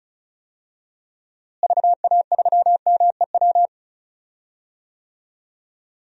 Morse code, heard as a keyed tone, about 36 wpm, 700 Hz. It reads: VA3MEW
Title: DXCC-100 Call Signs - 6 Characters - Repeated 3x - 35wpm